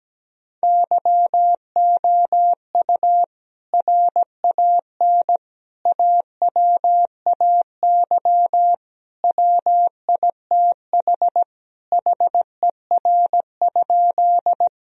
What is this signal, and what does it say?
Morse code, heard as a keyed tone, about 17 wpm, 700 Hz. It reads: YOU RAN AWAY WITH HER?
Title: Sentences from Top 1000 Words - Review Part 3 - 17wpm